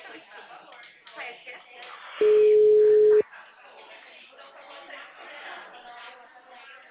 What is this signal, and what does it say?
Ringback tone (Europe, 425 Hz), with speech.